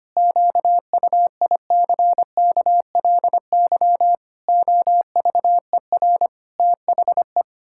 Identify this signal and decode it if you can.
Morse code, heard as a keyed tone, about 25 wpm, 700 Hz. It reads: QUICKLY OVER THE
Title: Pangram Challenge v2 - 25wpm